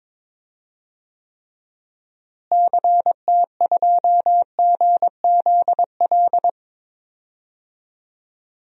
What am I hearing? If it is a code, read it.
Morse code, heard as a keyed tone, about 22 wpm, 700 Hz. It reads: CT2GZL
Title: DXCC-100 Call Signs - 6 Characters - Repeated 2x - 22wpm